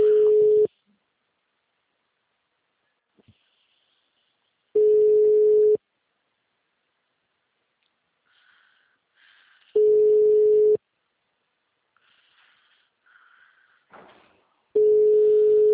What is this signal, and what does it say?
Ringback tone (Europe, 425 Hz).
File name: out- -2007- - - .14328